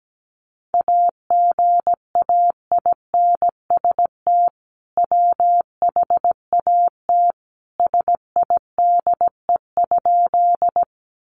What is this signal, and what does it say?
Morse code, heard as a keyed tone, about 17 wpm, 700 Hz. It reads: AGAINST WHAT SIDE?
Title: Sentences from Top 400 Words 17wpm